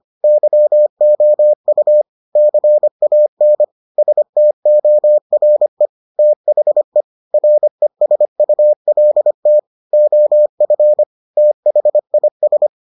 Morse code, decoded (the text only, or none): YOU CAN STORE THE RESULT OF THIS